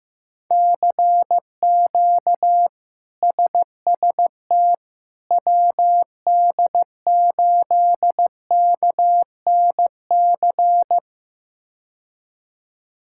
Morse code, heard as a keyed tone, about 15 wpm, 700 Hz. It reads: CQ SST WD8KNC
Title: K1USN SST QSO Practice - 15wpm